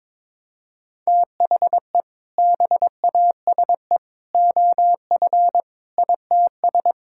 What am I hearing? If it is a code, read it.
Morse code, heard as a keyed tone, about 22 wpm, 700 Hz. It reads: THE BASE OF ITS
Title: Sentences from Top 1000 Words - Review Part 5 - 22wpm